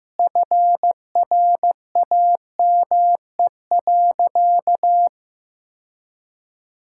Morse code, heard as a keyed tone, about 15 wpm, 700 Hz. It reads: FRAME.
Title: Sentences from Top 1000 Words - Review Part 6 - 15wpm